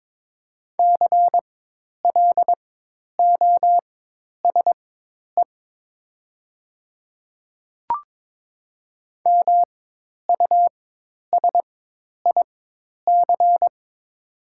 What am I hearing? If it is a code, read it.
Morse code, heard as a keyed tone, about 22 wpm, spaced as for 10 wpm, 700 Hz. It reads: CLOSE MUSIC
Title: 4x Character Spacing - Top 500 Words - 22wpm